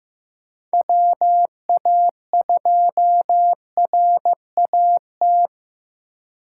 Morse code, heard as a keyed tone, about 15 wpm, 700 Hz. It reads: WA2RAT
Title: USA Call Signs - 6 Characters - 4x Character Spacing - 15wpm